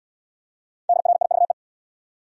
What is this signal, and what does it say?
Morse code, heard as a keyed tone, about 68 wpm, 700 Hz. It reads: DRIVE